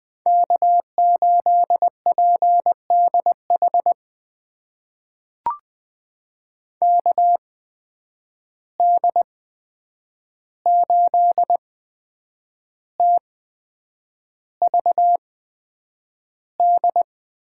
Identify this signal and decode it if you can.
Morse code, heard as a keyed tone, about 20 wpm, spaced as for 8 wpm, 700 Hz. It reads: K8PDH KD8TVD
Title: USA Call Signs - 8x Character Spacing - 20wpm